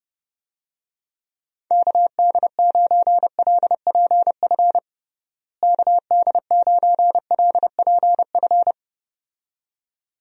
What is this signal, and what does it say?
Morse code, heard as a keyed tone, about 30 wpm, 700 Hz. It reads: KD9LPF KD9LPF
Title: USA Call Signs - 6 Characters - Repeated 2x - 30wpm